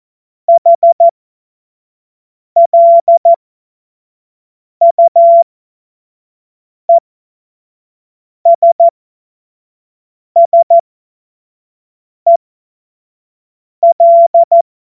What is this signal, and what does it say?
Morse code, heard as a keyed tone, about 14 wpm, spaced as for 5 wpm, 680 Hz. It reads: HLUESSEL